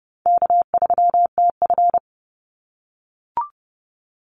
Morse code, heard as a keyed tone, about 30 wpm, 700 Hz. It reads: K3TF
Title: Easy Call Signs - 30wpm